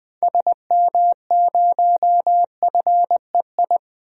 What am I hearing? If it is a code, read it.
Morse code, heard as a keyed tone, about 20 wpm, 700 Hz. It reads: SM0FEI
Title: DXCC-100 Call Signs - 6 Characters - 20wpm